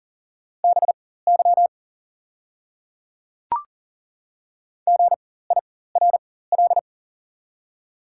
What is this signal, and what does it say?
Morse code, heard as a keyed tone, about 40 wpm, spaced as for 19 wpm, 700 Hz. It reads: BY GIRL
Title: 4x Character Spacing - Top 500 Words - 40wpm